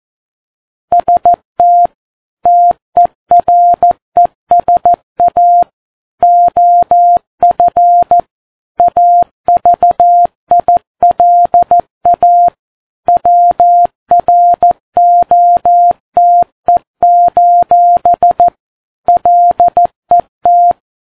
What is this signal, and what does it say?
Morse code, heard as a keyed tone, about 14 wpm, 700 Hz. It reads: ST TERESA OF AVILA WROTE: LET
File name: Ham Radio Sat